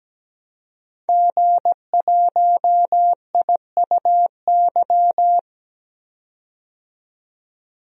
Morse code, heard as a keyed tone, about 17 wpm, 700 Hz. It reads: G1IUY
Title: DXCC-100 Call Signs - 5 Characters - 17wpm